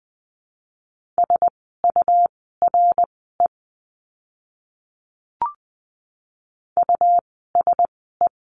Morse code, heard as a keyed tone, about 20 wpm, spaced as for 14 wpm, 700 Hz. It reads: SURE USE
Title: 101 Lesson 17f - Letter u - Words 20wpm